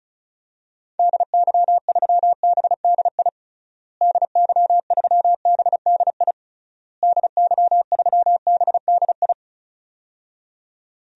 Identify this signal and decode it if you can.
Morse code, heard as a keyed tone, about 36 wpm, 700 Hz. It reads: DY3BDI DY3BDI DY3BDI
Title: DXCC-100 Call Signs - 6 Characters - Repeated 3x - 35wpm